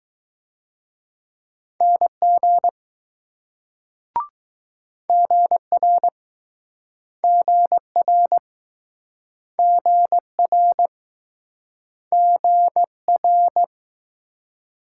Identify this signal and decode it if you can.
Morse code, heard as a keyed tone, about 20 wpm, 700 Hz. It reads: NG GR GR GR GR